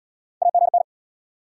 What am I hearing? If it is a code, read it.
Morse code, heard as a keyed tone, about 76 wpm, 700 Hz. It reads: AGN